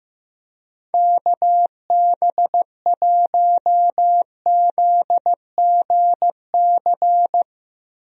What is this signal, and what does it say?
Morse code, heard as a keyed tone, about 15 wpm, 700 Hz. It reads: KB1ZGC
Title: USA Call Signs - 6 Characters - Repeated 2x - 15wpm